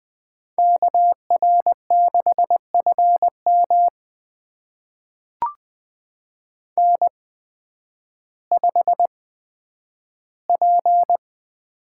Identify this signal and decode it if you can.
Morse code, heard as a keyed tone, about 20 wpm, spaced as for 11 wpm, 700 Hz. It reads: KR6FM N5P
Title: USA Call Signs - 5 Characters - 8x Character Spacing - 20wpm